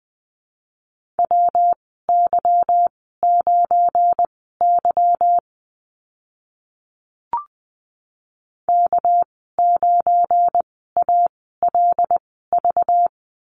Morse code, heard as a keyed tone, about 20 wpm, spaced as for 14 wpm, 700 Hz. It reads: WY9Y K9ALV
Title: 231 Lesson 37g - Letter x - Call Signs 20wpm